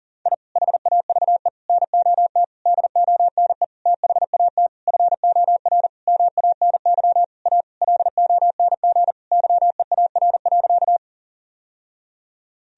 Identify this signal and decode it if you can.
Morse code, heard as a keyed tone, about 40 wpm, 700 Hz. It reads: I HAVE NOT DONE THAT FOR MANY A LONG YEAR.
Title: Sentences from Top 1000 Words - Review Part 1 - 40wpm